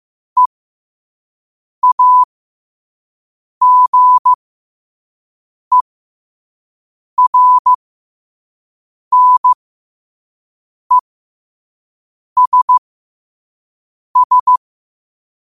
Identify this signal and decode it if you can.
Morse code, heard as a keyed tone, about 15 wpm, spaced as for 5 wpm, 1000 Hz. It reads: EAGERNESS
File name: qotdfiles / 5wpm